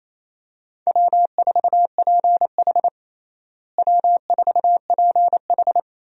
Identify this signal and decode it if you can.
Morse code, heard as a keyed tone, about 28 wpm, 700 Hz. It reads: W4PH W4PH
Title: USA Call Signs - 4 Characters - Repeated 2x - 28wpm